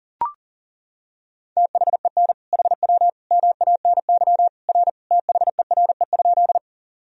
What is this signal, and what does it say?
Morse code, heard as a keyed tone, about 40 wpm, 700 Hz. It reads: THEN HW MANY R THERE?
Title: Sentences from Top 100 Words and Abbreviations 40wpm